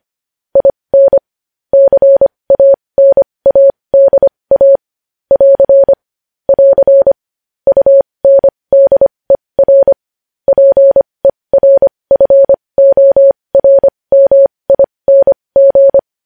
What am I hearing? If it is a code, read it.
Morse code, heard as a keyed tone, about 25 wpm, spaced as for 20 wpm, 550 Hz. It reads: IN CANADA <AR> <AR> UNDER PERFORMING
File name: podcast / News headlines-1x-25x20